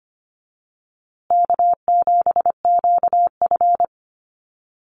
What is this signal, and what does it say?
Morse code, heard as a keyed tone, about 25 wpm, 700 Hz. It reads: K7QF
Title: USA Call Signs Repeated 2x 25wpm